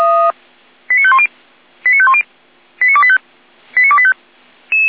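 Five-tone selective call (ZVEI), heard as quick runs of five tones, tones of about 70 ms. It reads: 89510 89510 89285 89285
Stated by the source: Received by SDR, ZVEI five-tone call 89262.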